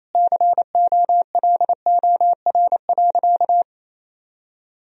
Morse code, heard as a keyed tone, about 28 wpm, 700 Hz. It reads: COLOR.
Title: Sentences from Top 1000 Words - Review Part 4 - 28wpm